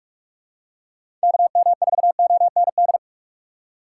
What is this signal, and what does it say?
Morse code, heard as a keyed tone, about 43 wpm, 700 Hz. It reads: KM4OND
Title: USA Call Signs - 6 Characters - 6x Character Spacing - 45wpm